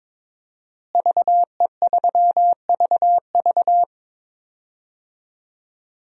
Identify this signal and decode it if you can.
Morse code, heard as a keyed tone, about 23 wpm, 700 Hz. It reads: VE3VV
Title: DXCC-100 Call Signs - 5 Characters - Repeated 3x - 22wpm